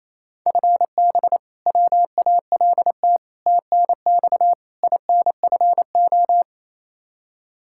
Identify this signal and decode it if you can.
Morse code, heard as a keyed tone, about 28 wpm, 700 Hz. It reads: FB WALT TNX INFO
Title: Practice QSO - Exchange 3 - More Info - 28wpm